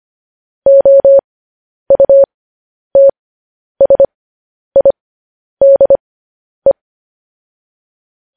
Morse code, heard as a keyed tone, about 25 wpm, spaced as for 10 wpm, 550 Hz. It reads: OUTSIDE
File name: podcast / News headlines-1x-25x10